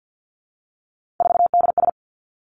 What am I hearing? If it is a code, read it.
Morse code, heard as a keyed tone, about 50 wpm, 700 Hz. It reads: 4DS
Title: Three Challenging Letter-Number Combinations 50wpm